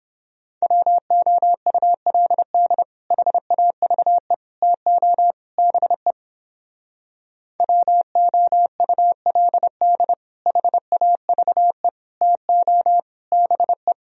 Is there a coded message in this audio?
Yes: Morse code from 0.6 to 13.9 s, heard as a keyed tone, about 29 wpm, 700 Hz. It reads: WOULD HAVE TO BE WOULD HAVE TO BE